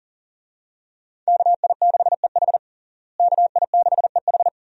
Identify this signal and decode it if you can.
Morse code, heard as a keyed tone, about 37 wpm, 700 Hz. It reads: KI6EH KI6EH
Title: USA Call Signs - 5 Characters - Repeated 3x - 40wpm